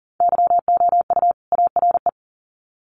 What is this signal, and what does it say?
Morse code, heard as a keyed tone, about 40 wpm, 700 Hz. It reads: YOU ARE
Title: Sets of 2 Words 40wpm